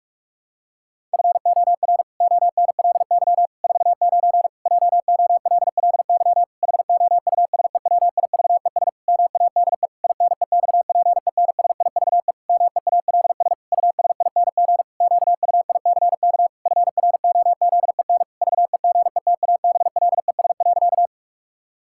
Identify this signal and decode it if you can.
Morse code, heard as a keyed tone, about 43 wpm, 700 Hz. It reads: FOR ONLY 49 JOLLY HOUSEWIVES MADE INEXPENSIVE MEALS USING QUICK FROZEN VEGETABLES.
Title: Pangram Challenge v2 - 45wpm